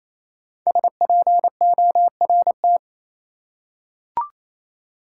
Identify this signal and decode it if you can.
Morse code, heard as a keyed tone, about 28 wpm, 700 Hz. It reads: SPORT
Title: Five Letter Words - 28wpm